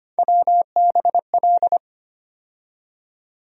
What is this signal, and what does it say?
Morse code, heard as a keyed tone, about 25 wpm, 700 Hz. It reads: WBL
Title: USA Call Sign Postfixes - 3 Letters 25wpm